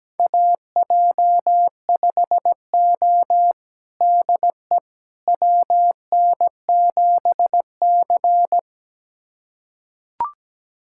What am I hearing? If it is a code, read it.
Morse code, heard as a keyed tone, about 17 wpm, 700 Hz. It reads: AJ5O DE WN7C
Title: USA Callsign DE Callsign - 4 Characters - 17wpm